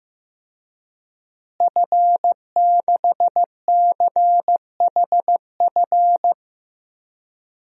Morse code, heard as a keyed tone, about 15 wpm, 700 Hz. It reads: F6CHF
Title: DXCC-100 Call Signs - 5 Characters - Repeated 3x - 15wpm